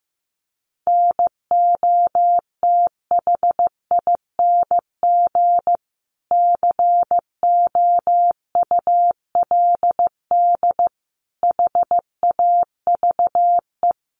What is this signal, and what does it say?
Morse code, heard as a keyed tone, about 14 wpm, 700 Hz. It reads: NOTHING COULD HAVE
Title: Sentences from Top 1300 Words 15wpm